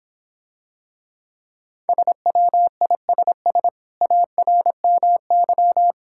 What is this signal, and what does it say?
Morse code, heard as a keyed tone, about 26 wpm, 700 Hz. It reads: SWISS ARMY